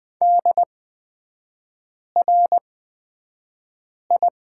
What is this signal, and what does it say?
Morse code, heard as a keyed tone, about 20 wpm, spaced as for 5 wpm, 700 Hz. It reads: DRI
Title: 224 Lesson 36e - BT - Three Characters 20wpm